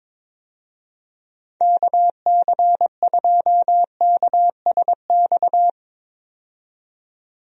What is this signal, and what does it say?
Morse code, heard as a keyed tone, about 22 wpm, 700 Hz. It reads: KC2KSX
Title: USA Call Signs - 6 Characters - 22wpm